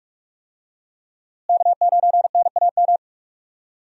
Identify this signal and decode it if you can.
Morse code, heard as a keyed tone, about 43 wpm, 700 Hz. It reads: K9NAM
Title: USA Call Signs - 5 Characters - Repeated 3x - 45wpm